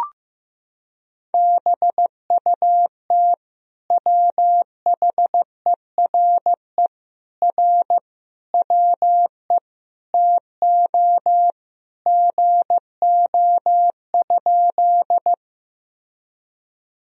Morse code, heard as a keyed tone, about 15 wpm, 700 Hz. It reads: BUT WHERE R WE TO GO?
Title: Sentences from Top 200 Words and Abbreviations 15wpm